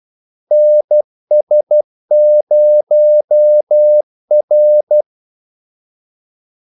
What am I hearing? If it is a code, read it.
Morse code, heard as a keyed tone, about 12 wpm, 600 Hz. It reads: NS0R
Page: Kaldesignaler 12 wpm